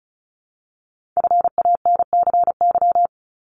Morse code, heard as a keyed tone, about 36 wpm, 700 Hz. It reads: FANCY
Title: Multi-letter Phonemes 35wpm